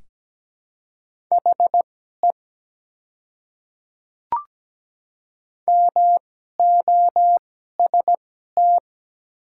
Morse code, heard as a keyed tone, about 17 wpm, spaced as for 12 wpm, 700 Hz. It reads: HE MOST
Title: 2x Character Spacing - Top 100 Words - 17wpm